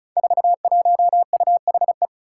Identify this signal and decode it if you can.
Morse code, heard as a keyed tone, about 36 wpm, 700 Hz. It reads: 41UHE
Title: Five Challenging Letter-Number Combinations 35wpm